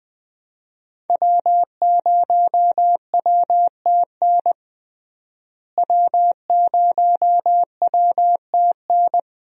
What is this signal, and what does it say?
Morse code, heard as a keyed tone, about 20 wpm, 700 Hz. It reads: W0WTN W0WTN
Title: USA Call Signs - 5 Characters - Repeated 3x - 20wpm